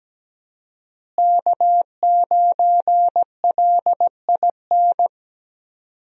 Morse code, heard as a keyed tone, about 17 wpm, 700 Hz. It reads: K9LIN
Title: USA Call Signs - 5 Characters - Repeated 3x - 17wpm